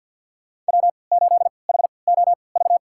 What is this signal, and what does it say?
Morse code, heard as a keyed tone, about 51 wpm, spaced as for 28 wpm, 700 Hz. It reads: W8HOV
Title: USA Call Signs - 3x Character Spacing - 50wpm